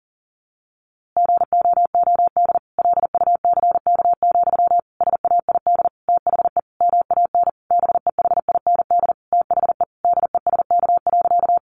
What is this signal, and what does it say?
Morse code, heard as a keyed tone, about 40 wpm, 700 Hz. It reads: GOOD LUCK, SAID THE MAN BEHIND THE DESK.
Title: A Sound Of Thunder 40wpm